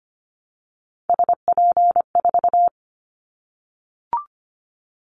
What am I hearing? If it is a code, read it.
Morse code, heard as a keyed tone, about 25 wpm, 700 Hz. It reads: SP4
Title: Callsign Prefixes for Top 25 DXCC - Repeated 3x - 25wpm